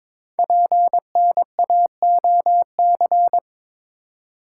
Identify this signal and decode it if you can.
Morse code, heard as a keyed tone, about 22 wpm, 700 Hz. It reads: PNAOC
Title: Code Groups - 5 Letters - 22wpm